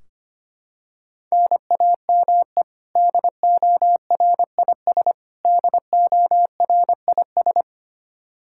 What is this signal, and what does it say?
Morse code, heard as a keyed tone, about 25 wpm, 700 Hz. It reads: NAME DORIS DORIS
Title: Practice QSO - Exchange 1 - Beginning - 25wpm